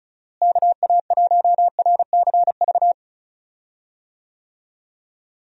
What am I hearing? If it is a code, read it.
Morse code, heard as a keyed tone, about 34 wpm, 700 Hz. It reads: KA1RCV
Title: USA Call Signs - 6 Characters - Repeated 2x - 35wpm